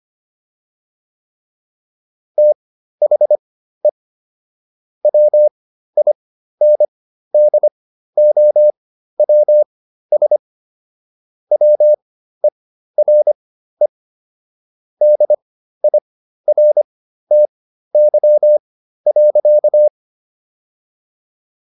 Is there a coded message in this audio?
Yes: Morse code from 2.4 to 19.9 s, heard as a keyed tone, about 25 wpm, spaced as for 13 wpm, 600 Hz. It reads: THE WINDOWS WERE DIRTY.